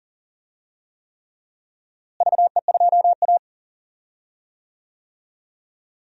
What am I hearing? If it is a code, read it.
Morse code, heard as a keyed tone, about 39 wpm, 700 Hz. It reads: VE2A
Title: DXCC-100 Call Signs - 4 Characters - Repeated 3x - 40wpm